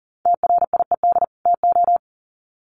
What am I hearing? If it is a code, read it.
Morse code, heard as a keyed tone, about 40 wpm, 700 Hz. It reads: TRIED TO
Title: Sets of 2 Words 40wpm